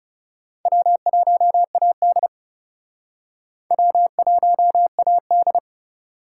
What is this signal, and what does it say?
Morse code, heard as a keyed tone, about 40 wpm, 700 Hz. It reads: W1AD W1AD